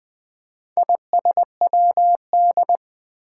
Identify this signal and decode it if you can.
Morse code, heard as a keyed tone, about 20 wpm, 700 Hz. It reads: ISWD
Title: Four Challenging Letter-Number Combinations 20wpm